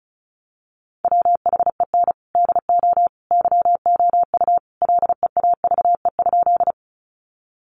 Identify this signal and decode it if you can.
Morse code, heard as a keyed tone, about 34 wpm, 700 Hz. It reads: WHEN DO YOU LEAVE?
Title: Sentences from Top 400 Words 35wpm